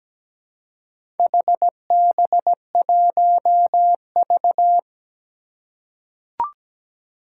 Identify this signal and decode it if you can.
Morse code, heard as a keyed tone, about 17 wpm, 700 Hz. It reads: HB1V
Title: Four Challenging Letter-Number Combinations 17wpm